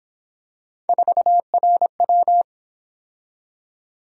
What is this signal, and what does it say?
Morse code, heard as a keyed tone, about 26 wpm, 700 Hz. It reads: 4RW
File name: Speed-Racing - Three Letter-Number Combinations - 22wpm